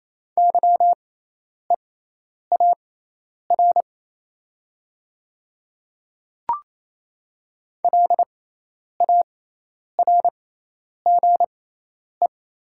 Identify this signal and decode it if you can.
Morse code, heard as a keyed tone, about 28 wpm, spaced as for 10 wpm, 700 Hz. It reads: YEAR LARGE
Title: 6x Character Spacing - Top 200 Words - 28wpm